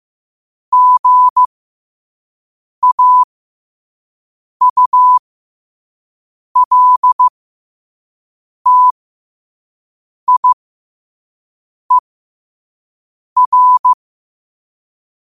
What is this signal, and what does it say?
Morse code, heard as a keyed tone, about 15 wpm, spaced as for 5 wpm, 1000 Hz. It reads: GAULTIER
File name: qotdfiles / 5wpm